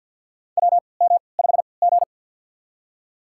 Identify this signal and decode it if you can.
Morse code, heard as a keyed tone, about 51 wpm, spaced as for 28 wpm, 700 Hz. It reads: WM5G
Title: USA Call Signs - 4 Characters - 3x Character Spacing - 50wpm